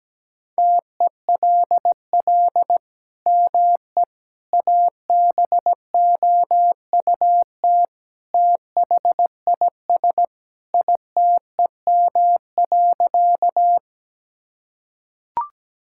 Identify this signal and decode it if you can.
Morse code, heard as a keyed tone, about 17 wpm, 700 Hz. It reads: TELL ME ABOUT THIS ITEM.
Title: Sentences from Top 1000 Words - Review Part 6 - 17wpm